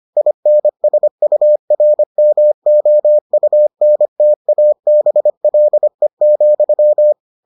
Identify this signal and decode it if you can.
Morse code, heard as a keyed tone, about 25 wpm, 600 Hz. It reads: INSURMOUNTABLE,